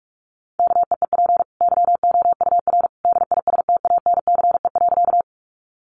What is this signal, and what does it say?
Morse code, heard as a keyed tone, about 44 wpm, 700 Hz. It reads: KEEP YOUR DISTANCE.
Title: Sentences from Top 1200 Words 45wpm